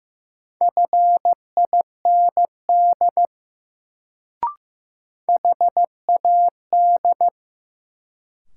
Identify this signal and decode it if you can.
Morse code, heard as a keyed tone, about 15 wpm, 700 Hz. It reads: FIND HAD
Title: Sentences from Top 100 Words - New Word Drill - 15wpm